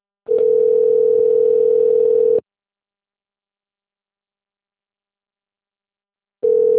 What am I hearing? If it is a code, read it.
Ringback tone (North America), 440 + 480 Hz.